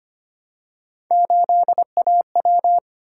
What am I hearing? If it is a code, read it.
Morse code, heard as a keyed tone, about 25 wpm, 700 Hz. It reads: 8AW
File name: Three Letter-Number Combinations - 25wpm